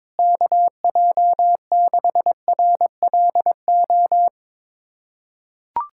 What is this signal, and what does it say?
Morse code, heard as a keyed tone, about 22 wpm, 700 Hz. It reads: KJ6RLO
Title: USA Callsign DE Callsign 22wpm